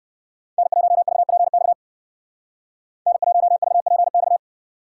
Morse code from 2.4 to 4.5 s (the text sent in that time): N1VPX